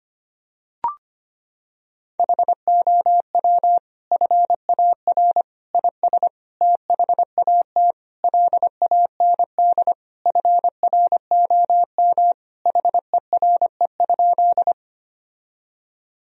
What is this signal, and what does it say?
Morse code, heard as a keyed tone, about 25 wpm, 700 Hz. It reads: HOW FAR IS THAT LAND FROM HERE?
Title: Sentences from Top 1000 Words - Review Part 1 - 25wpm